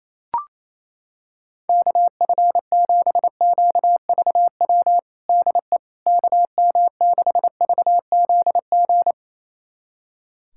Morse code, heard as a keyed tone, about 28 wpm, 700 Hz. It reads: KF7QVW DE KM6VZG
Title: USA Callsign DE Callsign 28wpm